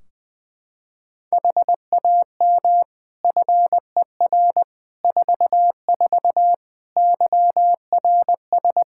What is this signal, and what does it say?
Morse code, heard as a keyed tone, about 21 wpm, 700 Hz. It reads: HAM FER 44 YRS
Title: Practice QSO - Exchange 3 - More Info - 20wpm